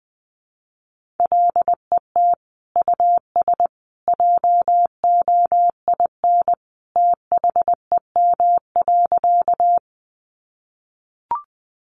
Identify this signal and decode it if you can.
Morse code, heard as a keyed tone, about 20 wpm, 700 Hz. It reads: LET US JOIN THEM.
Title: Sentences from Top 1400 Words 20wpm